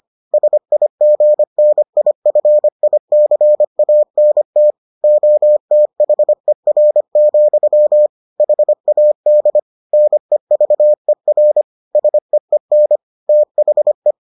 Morse code, heard as a keyed tone, about 25 wpm, 600 Hz. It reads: SIGNIFICANT OTHER, HAD NEVER SEEN THE